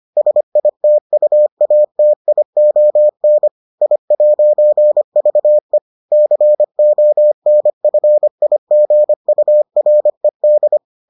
Morse code, heard as a keyed tone, about 25 wpm, 600 Hz. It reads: SITUATION I'VE CONFIGURED